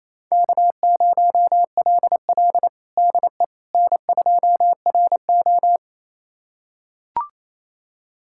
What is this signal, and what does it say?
Morse code, heard as a keyed tone, about 28 wpm, 700 Hz. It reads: K0LL DE N2RO
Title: USA Callsign DE Callsign - 4 Characters - 28wpm